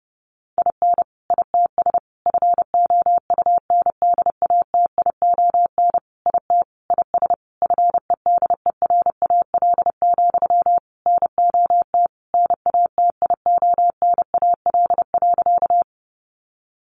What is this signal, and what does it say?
Morse code, heard as a keyed tone, about 30 wpm, 700 Hz. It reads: IN ITS FOUNDATION IT IS FEDERAL, NOT NATIONAL.
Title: Sentences from Top 1200 Words 30wpm